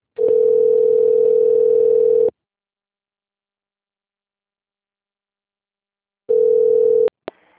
Ringback tone (North America), 440 + 480 Hz.